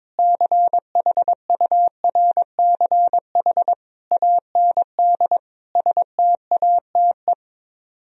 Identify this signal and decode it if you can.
Morse code, heard as a keyed tone, about 22 wpm, 700 Hz. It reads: CHURCH AND STATE
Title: Speed-Racing - Sets of 5 Words 22wpm